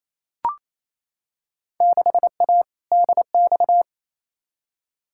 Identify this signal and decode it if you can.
Morse code, heard as a keyed tone, about 28 wpm, 700 Hz. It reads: 6A DX
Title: Field Day Exchanges Encore - 28wpm